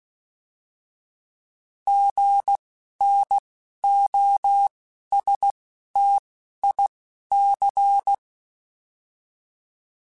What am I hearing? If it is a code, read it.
Morse code, heard as a keyed tone, about 16 wpm, spaced as for 11 wpm, 800 Hz. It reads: GNOSTIC
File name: Alphabetical / Alphabetical mp3s / Gs @10